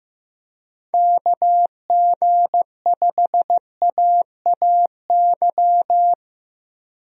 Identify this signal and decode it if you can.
Morse code, heard as a keyed tone, about 15 wpm, 700 Hz. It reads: KG5AAY
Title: USA Call Signs - 6 Characters - Repeated 3x - 15wpm